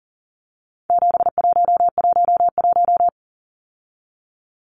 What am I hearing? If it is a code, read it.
Morse code, heard as a keyed tone, about 40 wpm, 700 Hz. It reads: 7111
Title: Numbers - 4-Digit 40wpm